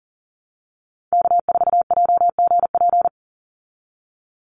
Morse code, heard as a keyed tone, about 40 wpm, 700 Hz. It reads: K4JGP
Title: USA Call Signs 40wpm